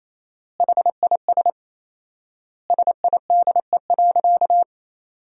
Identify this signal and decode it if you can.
Morse code, heard as a keyed tone, about 28 wpm, 700 Hz. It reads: HIS SIDE.
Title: 4x Word-Spacing - Sentences from Top 400 Words - 28wpm